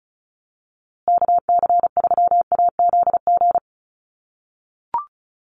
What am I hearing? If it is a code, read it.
Morse code, heard as a keyed tone, about 34 wpm, 700 Hz. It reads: KC3AZG
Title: USA Call Signs Repeated 2x 35wpm